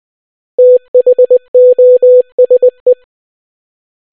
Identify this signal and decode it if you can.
Morse code, heard as a keyed tone, about 20 wpm, 500 Hz. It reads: THOSE